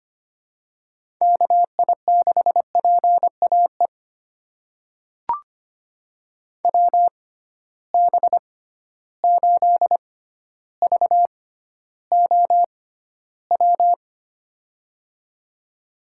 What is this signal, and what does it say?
Morse code, heard as a keyed tone, about 25 wpm, spaced as for 13 wpm, 700 Hz. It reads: KI6PAE WB8VOW
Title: USA Call Signs - 6 Characters - 6x Character Spacing - 25wpm